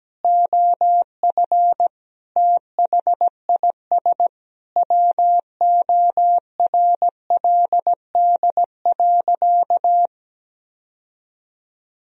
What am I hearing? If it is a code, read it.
Morse code, heard as a keyed tone, about 17 wpm, 700 Hz. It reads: OF THIS WORLD.
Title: Sentences from Top 1000 Words - Review Part 1 - 17wpm